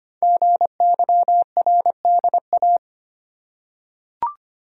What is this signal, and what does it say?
Morse code, heard as a keyed tone, about 25 wpm, 700 Hz. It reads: GYRDA
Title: Code Groups - 5 Letters - 25wpm